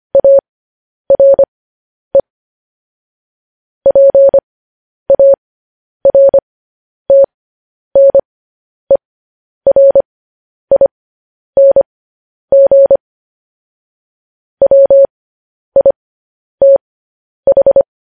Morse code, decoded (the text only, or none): ARE PARTNERING WITH